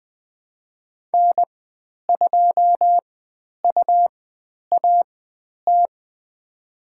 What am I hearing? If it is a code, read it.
Morse code, heard as a keyed tone, about 20 wpm, spaced as for 10 wpm, 700 Hz. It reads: N2UAT
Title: 102 Lesson 17g - Letter u - Call Signs 20wpm